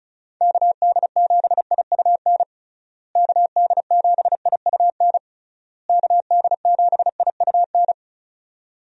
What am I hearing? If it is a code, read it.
Morse code, heard as a keyed tone, about 36 wpm, 700 Hz. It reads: KD7IUN KD7IUN KD7IUN
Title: USA Call Signs - 6 Characters - Repeated 3x - 35wpm